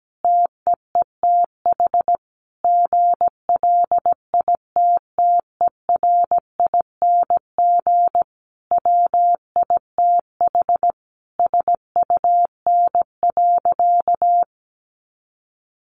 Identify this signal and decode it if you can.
Morse code, heard as a keyed tone, about 17 wpm, 700 Hz. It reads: TEETH GLITTERING WITH SUN.
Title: A Sound Of Thunder 17wpm